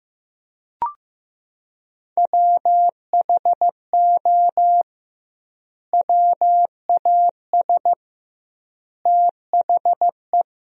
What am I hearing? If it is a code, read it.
Morse code, heard as a keyed tone, about 15 wpm, 700 Hz. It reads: WHO WAS THE
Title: 2x Word-Spacing - Sentences from Top 300 Words - 15wpm